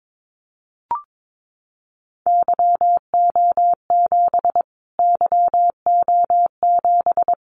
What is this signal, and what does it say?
Morse code, heard as a keyed tone, about 22 wpm, 700 Hz. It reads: YO7 YO7
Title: Callsign Prefixes for Top 25 DXCC - Repeated 3x - 22wpm